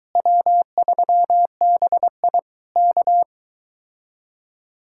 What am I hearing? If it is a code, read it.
Morse code, heard as a keyed tone, about 23 wpm, 700 Hz. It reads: W3BI K